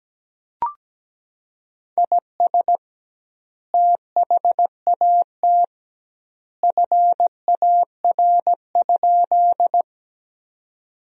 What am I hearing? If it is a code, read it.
Morse code, heard as a keyed tone, about 17 wpm, 700 Hz. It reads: IS THAT FAR?
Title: 2x Word-Spacing - Sentences from Top 300 Words - 17wpm